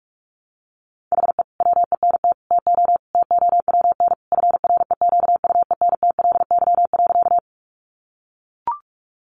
Morse code, heard as a keyed tone, about 44 wpm, 700 Hz. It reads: HE WENT TO TOWN FREQUENTLY.
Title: Sentences from Top 1300 Words 45wpm